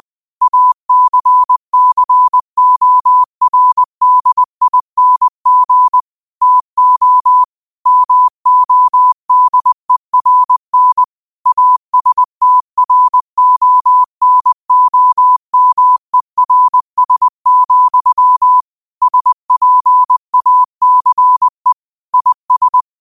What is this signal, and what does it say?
Morse code, heard as a keyed tone, about 20 wpm, 1000 Hz. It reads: ACCORDING TO MODERN ASTRONOMERS, SPACE IS
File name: qotdfiles / 20wpm